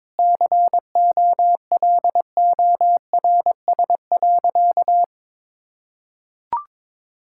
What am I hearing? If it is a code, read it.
Morse code, heard as a keyed tone, about 21 wpm, 700 Hz. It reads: COLORS.
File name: practice / 22wpm / Sentences with 4-Word Phrases - 22wpm